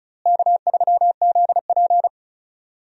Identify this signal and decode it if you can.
Morse code, heard as a keyed tone, about 36 wpm, 700 Hz. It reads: K3ZP
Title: USA Call Signs - 4 Characters - Repeated 3x - 35wpm